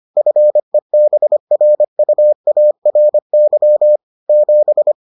Morse code, heard as a keyed tone, about 25 wpm, 600 Hz. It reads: FEBRUARY 7